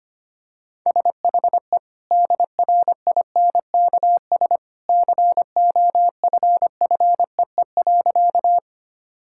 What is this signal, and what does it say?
Morse code, heard as a keyed tone, about 25 wpm, 690 Hz. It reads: SHE DRINKS COFFEE.
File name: Easy Sentences with Hints - 25wpm